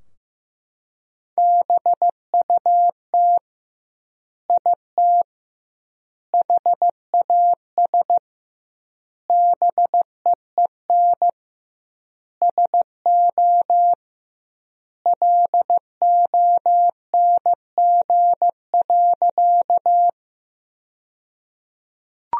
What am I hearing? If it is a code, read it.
Morse code, heard as a keyed tone, about 16 wpm, 700 Hz. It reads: BUT IT HAS BEEN SO LONG.
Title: 2x Word-Spacing - Sentences from Top 100 Words - 15wpm